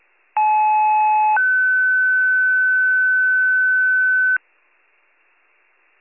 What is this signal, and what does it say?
Two-tone page, heard as a one-second tone, then a three-second tone, about 860 then 1500 Hz, with noise. It is heard from 0.4 to 4.4 s.